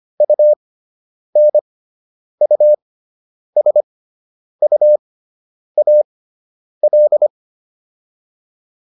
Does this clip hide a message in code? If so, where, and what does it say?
Yes: Morse code from 0.2 to 7.3 s, heard as a keyed tone, about 25 wpm, spaced as for 9 wpm, 600 Hz. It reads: UNUSUAL